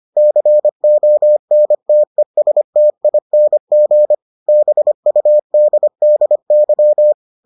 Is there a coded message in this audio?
Yes: Morse code from 0.2 to 7.1 s, heard as a keyed tone, about 25 wpm, 600 Hz. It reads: CONTESTING BUDDY